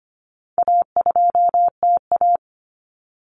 Morse code, heard as a keyed tone, about 25 wpm, 700 Hz. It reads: A2TA
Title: Common Cut Numbers - 4-Digit 25wpm